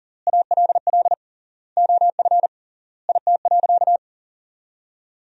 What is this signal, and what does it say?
Morse code, heard as a keyed tone, about 40 wpm, 700 Hz. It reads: ALL OF IT.
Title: 3x Word-Spacing - Sentences from Top 500 Words - 40wpm